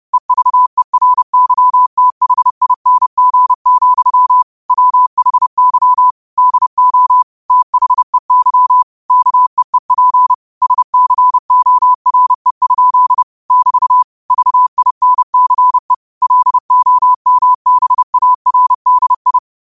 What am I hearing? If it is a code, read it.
Morse code, heard as a keyed tone, about 30 wpm, 1000 Hz. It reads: EVERYTHING, WHY DO THEY KEEP SCORE? <BT> VINCE LOMBARDI
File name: 30wpm